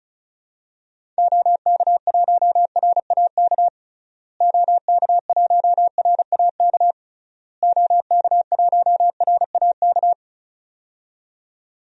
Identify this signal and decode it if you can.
Morse code, heard as a keyed tone, about 36 wpm, 700 Hz. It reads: OK1RAK OK1RAK OK1RAK
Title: DXCC-100 Call Signs - 6 Characters - Repeated 3x - 35wpm